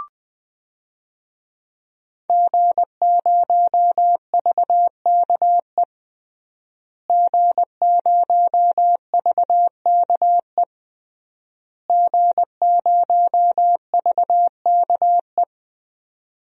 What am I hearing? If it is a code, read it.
Morse code, heard as a keyed tone, about 21 wpm, 700 Hz. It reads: G0VKE G0VKE G0VKE
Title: DXCC-100 Call Signs - 5 Characters - Repeated 3x - 20wpm